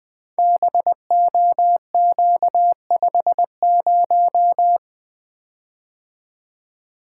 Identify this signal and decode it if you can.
Morse code, heard as a keyed tone, about 20 wpm, 700 Hz. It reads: BOQ50
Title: Code Groups - 5 Letter:Numbers - 20wpm